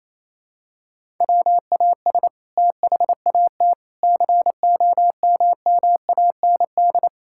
Morse code, heard as a keyed tone, about 27 wpm, 700 Hz. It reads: WAS THAT COMMAND
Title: Sentences from Top 1600 Words 28wpm